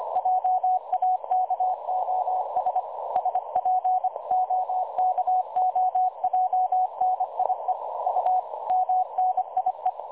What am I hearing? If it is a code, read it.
Morse code, heard as a keyed tone, about 25 wpm, 740 Hz. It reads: JAKOS SPOKOJNIE U MNIE
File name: dlazaawansowanych / nagraniazpasma / websdr recording 3565.0kHz